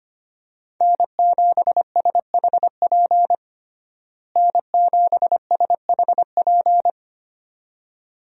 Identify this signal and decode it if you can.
Morse code, heard as a keyed tone, about 25 wpm, 700 Hz. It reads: N7SHP N7SHP
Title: USA Call Signs - 5 Characters - Repeated 2x - 25wpm